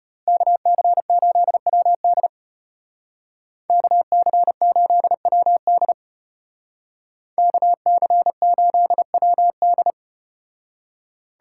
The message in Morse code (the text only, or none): KC8WD KC8WD KC8WD